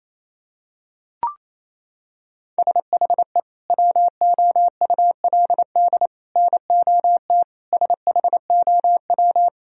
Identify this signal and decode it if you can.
Morse code, heard as a keyed tone, about 28 wpm, 700 Hz. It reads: SHE WOULD NOT SHOW
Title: Sentences from Top 1000 Words - Review Part 3 - 28wpm